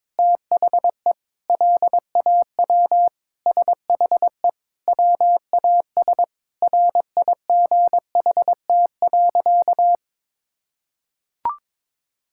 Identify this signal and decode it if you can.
Morse code, heard as a keyed tone, about 22 wpm, 700 Hz. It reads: THE LAW SHE WAS RIGHT.
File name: practice / 22wpm / Sentences with 4-Word Phrases - 22wpm